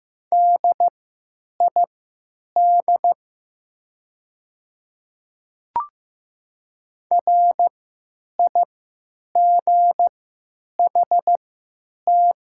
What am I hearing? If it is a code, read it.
Morse code, heard as a keyed tone, about 15 wpm, spaced as for 9 wpm, 700 Hz. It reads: DID RIGHT
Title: 3x Character Spacing - Top 300 Words - 15wpm